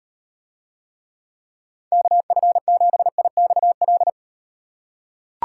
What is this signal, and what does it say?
Morse code, heard as a keyed tone, about 38 wpm, 700 Hz. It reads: KF7IXL